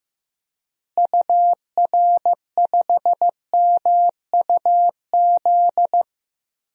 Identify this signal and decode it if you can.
Morse code, heard as a keyed tone, about 15 wpm, 700 Hz. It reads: UR5MUZ
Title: DXCC-100 Call Signs - 6 Characters - Repeated 2x - 15wpm